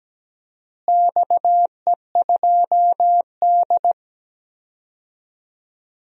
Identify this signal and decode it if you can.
Morse code, heard as a keyed tone, about 17 wpm, 700 Hz. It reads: XE2D
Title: DXCC-100 Call Signs - 4 Characters - Repeated 3x - 17wpm